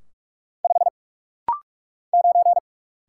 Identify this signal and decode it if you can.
Morse code, heard as a keyed tone, about 44 wpm, 700 Hz. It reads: 5 9
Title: Single Letter-Number - Rapid-Fire 45wpm